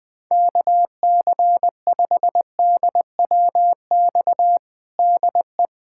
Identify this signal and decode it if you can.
Morse code, heard as a keyed tone, about 21 wpm, 700 Hz. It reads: KC5DWX DE
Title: USA Callsign DE Callsign - 6 Characters - 20wpm